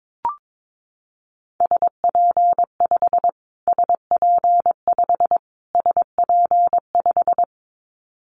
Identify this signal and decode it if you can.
Morse code, heard as a keyed tone, about 22 wpm, 700 Hz. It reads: SP5 SP5 SP5
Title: Callsign Prefixes for Top 25 DXCC - Repeated 3x - 22wpm